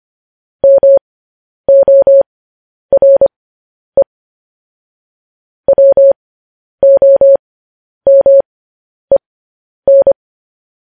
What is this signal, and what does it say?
Morse code, heard as a keyed tone, about 25 wpm, spaced as for 10 wpm, 550 Hz. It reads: MORE WOMEN
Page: news headlines in morse code at 10 wpm